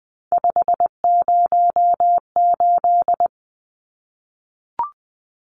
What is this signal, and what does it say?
Morse code, heard as a keyed tone, about 20 wpm, 700 Hz. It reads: 508
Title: Numbers - 3-Digit 20wpm